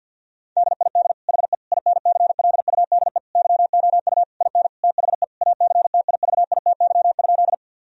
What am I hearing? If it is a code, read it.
Morse code, heard as a keyed tone, about 50 wpm, 700 Hz. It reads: DID HE INCLUDE YOU IN THE ACTIVITY?
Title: Sentences from Top 1000 Words - Review Part 4 - 50wpm